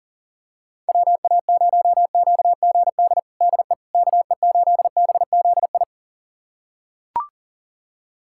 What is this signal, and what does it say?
Morse code, heard as a keyed tone, about 40 wpm, 700 Hz. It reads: WA0QGD DE KE8BZI
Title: USA Callsign DE Callsign 40wpm